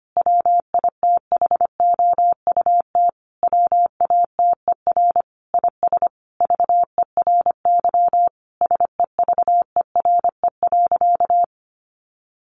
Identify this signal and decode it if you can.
Morse code, heard as a keyed tone, about 25 wpm, 700 Hz. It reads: WITHOUT WATER IS VERY SEVERE.
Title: Sentences from Top 1500 Words 25wpm